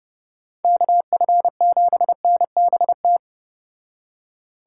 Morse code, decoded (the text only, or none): KF7NBT